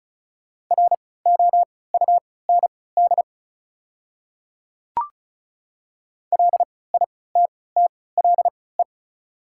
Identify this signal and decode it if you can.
Morse code, heard as a keyed tone, about 36 wpm, spaced as for 20 wpm, 690 Hz. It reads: ROUND LITTLE
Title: 3x Character Spacing - Top 300 Words - 35wpm